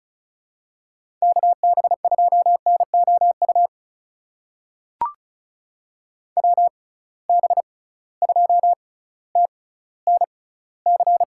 Morse code, heard as a keyed tone, about 36 wpm, spaced as for 18 wpm, 700 Hz. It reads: KB2NOU WB2TNC
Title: USA Call Signs - 6 Characters - 6x Character Spacing - 35wpm